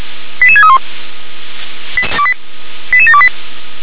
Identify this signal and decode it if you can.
Five-tone selective call (ZVEI), heard as quick runs of five tones, tones of about 70 ms. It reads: 88631 88728
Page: SDR/ZVEI